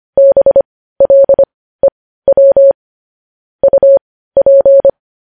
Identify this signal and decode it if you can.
Morse code, heard as a keyed tone, about 25 wpm, spaced as for 15 wpm, 550 Hz. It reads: BLEW UP
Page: news-headlines-in-morse-code-at-15-wpm